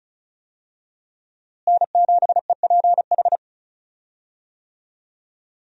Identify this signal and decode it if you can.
Morse code, heard as a keyed tone, about 34 wpm, 700 Hz. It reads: N7EPH
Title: USA Call Signs - 5 Characters - Repeated 2x - 35wpm